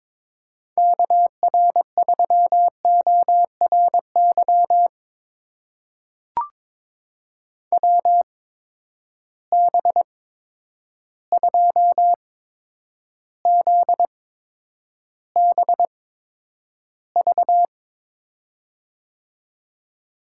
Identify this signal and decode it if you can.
Morse code, heard as a keyed tone, about 22 wpm, spaced as for 10 wpm, 700 Hz. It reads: KR3ORY WB2ZBV
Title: USA Call Signs - 6 Characters - 8x Character Spacing - 22wpm